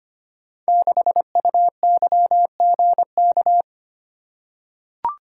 Morse code, heard as a keyed tone, about 25 wpm, 700 Hz. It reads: 6UYGK
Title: Code Groups - 5 Letter:Numbers - 25wpm